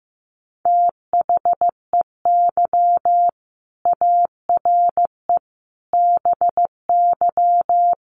Morse code, heard as a keyed tone, about 15 wpm, 700 Hz. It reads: THEY ARE BY
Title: Sentences from Top 800 Words 15wpm